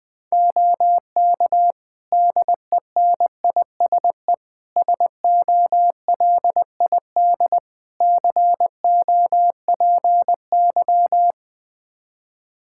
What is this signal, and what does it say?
Morse code, heard as a keyed tone, about 20 wpm, 700 Hz. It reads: OK DENISE SOLID COPY
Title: Practice QSO - Exchange 3 - More Info - 20wpm